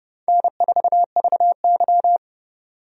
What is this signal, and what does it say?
Morse code, heard as a keyed tone, about 32 wpm, 700 Hz. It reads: N4VY
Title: USA Call Signs - 4 Characters - 30wpm